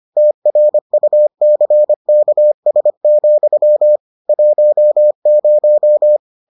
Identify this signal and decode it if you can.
Morse code, heard as a keyed tone, about 25 wpm, 600 Hz. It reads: TRUCKS, 10